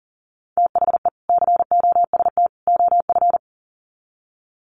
Morse code, heard as a keyed tone, about 40 wpm, 700 Hz. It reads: THE COST OF
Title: Sets of 3 Words 40wpm